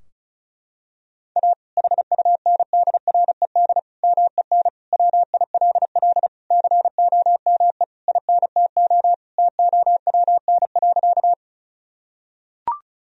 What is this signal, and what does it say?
Morse code, heard as a keyed tone, about 36 wpm, 700 Hz. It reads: A HUNDRED MEN WILL COME INTO TOWN.
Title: Sentences from Top 1000 Words - Review Part 2 - 35wpm